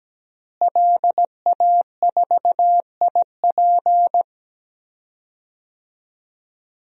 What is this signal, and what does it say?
Morse code, heard as a keyed tone, about 17 wpm, 700 Hz. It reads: LA4IP
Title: DXCC-100 Call Signs - 5 Characters - Repeated 2x - 17wpm